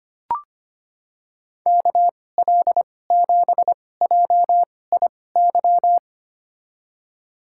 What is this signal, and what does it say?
Morse code, heard as a keyed tone, about 25 wpm, spaced as for 18 wpm, 700 Hz. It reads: KL7JIY
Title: USA Call Signs - 6 Characters - 2x Character Spacing - 25wpm